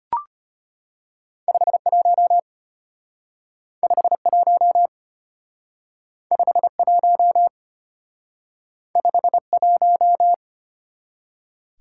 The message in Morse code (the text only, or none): 51 51 51 51